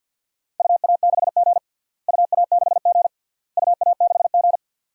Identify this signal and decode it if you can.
Morse code, heard as a keyed tone, about 46 wpm, 700 Hz. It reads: UA6G UA6G UA6G
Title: DXCC-100 Call Signs - 4 Characters - Repeated 3x - 50wpm